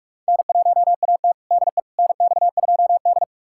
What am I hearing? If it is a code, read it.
Morse code, heard as a keyed tone, about 42 wpm, 700 Hz. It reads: N1AT DE NX2D
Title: USA Callsign DE Callsign - 4 Characters - 45wpm